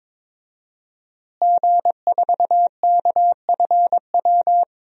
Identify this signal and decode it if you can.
Morse code, heard as a keyed tone, about 22 wpm, 700 Hz. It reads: G4KFW
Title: DXCC-100 Call Signs - 5 Characters - Repeated 2x - 22wpm